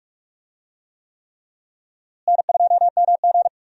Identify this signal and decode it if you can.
Morse code, heard as a keyed tone, about 46 wpm, 700 Hz. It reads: N2MG
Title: USA Call Signs - 4 Characters - Repeated 3x - 45wpm